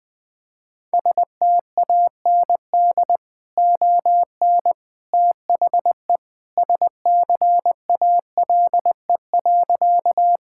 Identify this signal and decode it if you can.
Morse code, heard as a keyed tone, about 20 wpm, 700 Hz. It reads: STAND ON THE SCALE.
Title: Sentences from Top 1000 Words - Review Part 5 - 20wpm